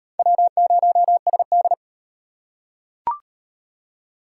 Morse code, heard as a keyed tone, about 34 wpm, 700 Hz. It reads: W0SD